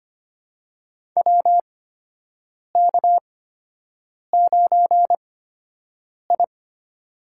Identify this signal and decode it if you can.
Morse code, heard as a keyed tone, about 25 wpm, spaced as for 7 wpm, 700 Hz. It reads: WK9I
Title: USA Call Signs - 4 Characters - 8x Character Spacing - 25wpm